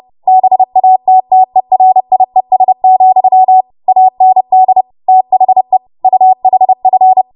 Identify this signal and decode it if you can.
Morse code, heard as a keyed tone, about 30 wpm, 750 Hz. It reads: BATTERIES, AND THE UHF
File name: Morse / 30WPM